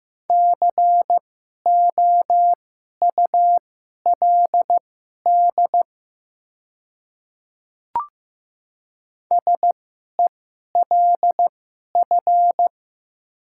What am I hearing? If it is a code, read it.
Morse code, heard as a keyed tone, about 15 wpm, spaced as for 11 wpm, 700 Hz. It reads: COULD SELF
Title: 2x Character Spacing - Top 300 Words - 15wpm